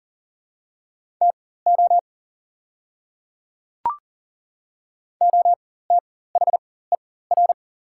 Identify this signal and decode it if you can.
Morse code, heard as a keyed tone, about 40 wpm, spaced as for 19 wpm, 690 Hz. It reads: TO OTHER
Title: 4x Character Spacing - Top 100 Words - 40wpm